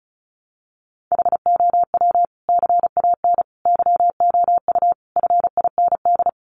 Morse code, heard as a keyed tone, about 34 wpm, 700 Hz. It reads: HOW CAN YOU FIND
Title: Sentences from Top 200 Words 35wpm